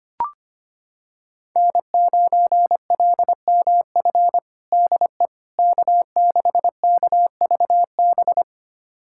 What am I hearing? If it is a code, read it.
Morse code, heard as a keyed tone, about 25 wpm, 700 Hz. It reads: N9LMF DE K6KVB
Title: USA Callsign DE Callsign - 5 Characters - 25wpm